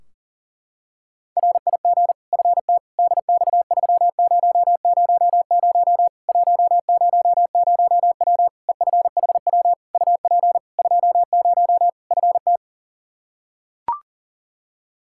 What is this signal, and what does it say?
Morse code, heard as a keyed tone, about 40 wpm, 700 Hz. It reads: RIG FT DX3000 100W EFHW UP 20 FT
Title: K1USN SST QSO Practice 40wpm